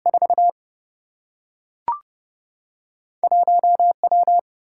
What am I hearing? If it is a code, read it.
Morse code, heard as a keyed tone, about 30 wpm, 700 Hz. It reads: 4 1 W
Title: Single-Letter Single-Number 30wpm